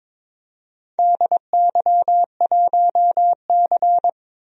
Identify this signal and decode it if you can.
Morse code, heard as a keyed tone, about 22 wpm, 700 Hz. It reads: DY1C
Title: DXCC-100 Call Signs - 4 Characters - 22wpm